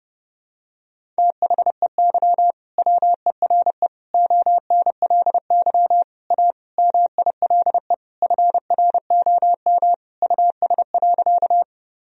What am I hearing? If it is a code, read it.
Morse code, heard as a keyed tone, about 30 wpm, 700 Hz. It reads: THEY WERE ONLY A MILE FROM US.
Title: Sentences from Top 1000 Words - Review Part 1 - 30wpm